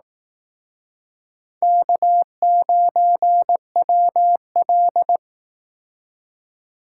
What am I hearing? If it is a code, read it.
Morse code, heard as a keyed tone, about 19 wpm, 700 Hz. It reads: K9WL